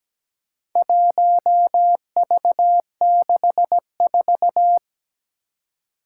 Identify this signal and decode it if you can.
Morse code, heard as a keyed tone, about 17 wpm, 700 Hz. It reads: 1V64
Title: Four Challenging Letter-Number Combinations 17wpm